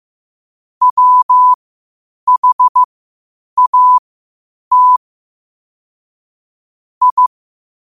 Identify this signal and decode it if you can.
Morse code, heard as a keyed tone, about 15 wpm, spaced as for 8 wpm, 1000 Hz. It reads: WHAT I
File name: qotdfiles / 8wpm